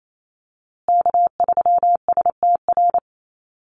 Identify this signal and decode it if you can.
Morse code, heard as a keyed tone, about 28 wpm, 700 Hz. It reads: K3STR
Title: USA Call Signs Repeated 2x 28wpm